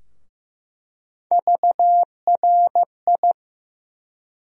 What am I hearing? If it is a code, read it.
Morse code, heard as a keyed tone, about 15 wpm, 700 Hz. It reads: VRI
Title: USA Call Sign Postfixes - 3 Letters 15wpm